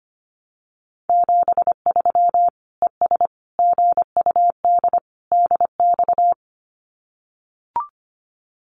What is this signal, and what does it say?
Morse code, heard as a keyed tone, about 25 wpm, 700 Hz. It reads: 73 ES GUD DX
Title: Common QSO Phrases 25wpm